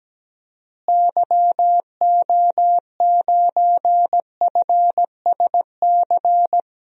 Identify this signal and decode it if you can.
Morse code, heard as a keyed tone, about 17 wpm, 700 Hz. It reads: YO9FSC
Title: DXCC-100 Call Signs - 6 Characters - Repeated 2x - 17wpm